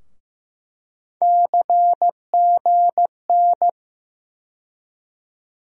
Morse code, heard as a keyed tone, about 15 wpm, 700 Hz. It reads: CGN
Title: USA Call Sign Postfixes - 3 Letters 15wpm